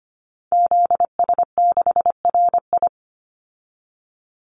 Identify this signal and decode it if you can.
Morse code, heard as a keyed tone, about 25 wpm, 700 Hz. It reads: ZS6RI
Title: Call Signs - 25wpm